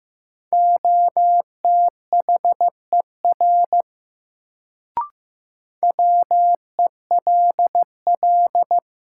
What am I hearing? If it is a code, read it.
Morse code, heard as a keyed tone, about 15 wpm, 700 Hz. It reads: OTHER WELL
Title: Sets of 2 Words - New Word Drill - 15wpm